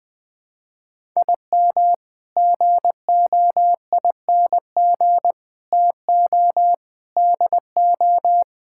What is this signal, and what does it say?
Morse code, heard as a keyed tone, about 20 wpm, 700 Hz. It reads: IM GOING TO DO
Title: Sentences with 8-Word Phrases - 20wpm